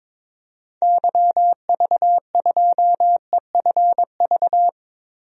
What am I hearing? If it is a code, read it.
Morse code, heard as a keyed tone, about 21 wpm, 700 Hz. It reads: YV2EFV
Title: DXCC-100 Call Signs - 6 Characters - Repeated 2x - 22wpm